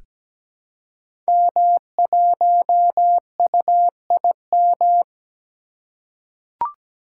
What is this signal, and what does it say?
Morse code, heard as a keyed tone, about 17 wpm, 700 Hz. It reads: M1UIM
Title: Code Groups - 5 Letter:Numbers - 17wpm